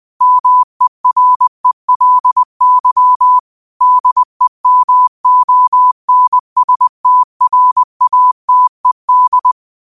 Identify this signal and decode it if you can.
Morse code, heard as a keyed tone, about 20 wpm, 1000 Hz. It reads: MERELY DEMONSTRATED
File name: qotdfiles / 20wpm